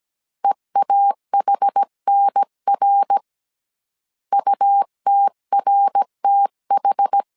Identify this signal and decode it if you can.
Morse code, heard as a keyed tone, about 17 wpm, 800 Hz. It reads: EAHNR UTRTH
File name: CW / DET / 1 3 1500C S A 16wpm 2141